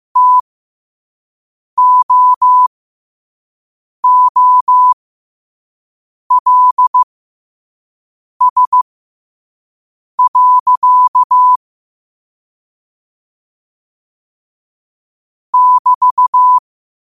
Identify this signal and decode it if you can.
Morse code, heard as a keyed tone, about 15 wpm, spaced as for 5 wpm, 1000 Hz. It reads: TOOLS. <BT>
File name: qotdfiles / 5wpm